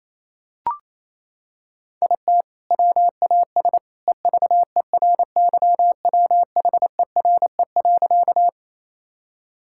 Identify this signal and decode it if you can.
Morse code, heard as a keyed tone, about 28 wpm, 700 Hz. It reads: IT WAS EVERYWHERE.
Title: Sentences from Top 1600 Words 28wpm